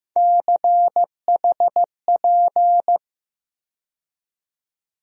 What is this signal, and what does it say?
Morse code, heard as a keyed tone, about 15 wpm, 700 Hz. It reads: CHP
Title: USA Call Sign Postfixes - 3 Letters 15wpm